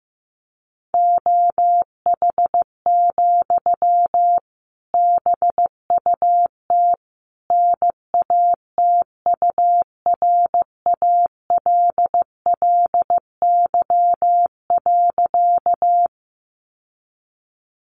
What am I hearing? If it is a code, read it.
Morse code, heard as a keyed tone, about 14 wpm, 700 Hz. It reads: OH, BUT NATURALLY.
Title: Sentences from Top 1300 Words 15wpm